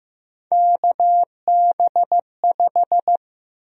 Morse code, heard as a keyed tone, about 15 wpm, 700 Hz. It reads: KB5
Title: USA Call Sign Prefixes - 15wpm